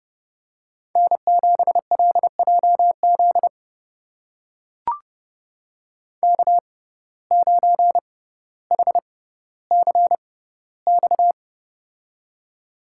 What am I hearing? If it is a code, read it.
Morse code, heard as a keyed tone, about 31 wpm, spaced as for 16 wpm, 700 Hz. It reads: N7LJZ K9HCX
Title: USA Call Signs - 5 Characters - 6x Character Spacing - 30wpm